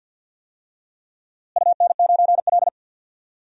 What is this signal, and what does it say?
Morse code, heard as a keyed tone, about 52 wpm, 700 Hz. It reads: UN9L
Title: DXCC-100 Call Signs - 4 Characters - Repeated 3x - 50wpm